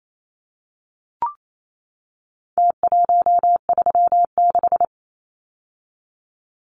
Morse code, heard as a keyed tone, about 28 wpm, 700 Hz. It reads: T136
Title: Numbers - Contest Style - 4-Digit 28wpm